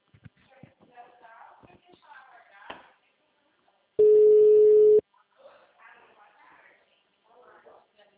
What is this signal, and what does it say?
Ringback tone (Europe, 425 Hz), with speech.